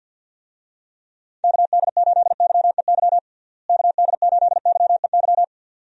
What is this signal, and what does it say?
Morse code, heard as a keyed tone, about 52 wpm, 700 Hz. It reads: KD8YEY KD8YEY
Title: USA Call Signs - 6 Characters - Repeated 2x - 50wpm